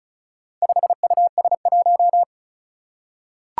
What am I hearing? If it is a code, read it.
Morse code, heard as a keyed tone, about 36 wpm, 700 Hz. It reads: 5US1
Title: Four Challenging Letter-Number Combinations 35wpm